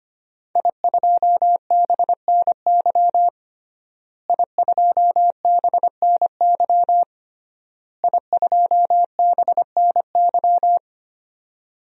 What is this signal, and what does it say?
Morse code, heard as a keyed tone, about 25 wpm, 700 Hz. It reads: I2BNY I2BNY I2BNY
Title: DXCC-100 Call Signs - 5 Characters - Repeated 3x - 25wpm